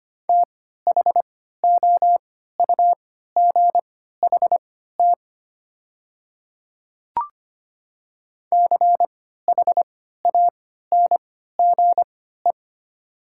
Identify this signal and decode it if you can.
Morse code, heard as a keyed tone, about 25 wpm, spaced as for 14 wpm, 700 Hz. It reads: THOUGHT CHANGE
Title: 3x Character Spacing - Top 300 Words - 25wpm